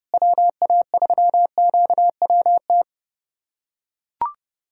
Morse code, heard as a keyed tone, about 32 wpm, 700 Hz. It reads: WA3QWT
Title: USA Callsign DE Callsign - 6 Characters - 30wpm